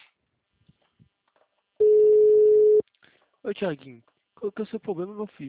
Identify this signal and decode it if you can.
Ringback tone (Europe, 425 Hz), with speech.